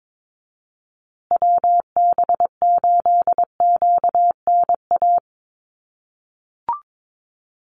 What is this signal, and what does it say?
Morse code, heard as a keyed tone, about 22 wpm, 700 Hz. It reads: WB8QNA
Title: USA Call Signs Repeated 3x 22wpm